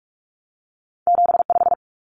Morse code, heard as a keyed tone, about 46 wpm, 700 Hz. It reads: 75
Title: Numbers - 2-Digit 45wpm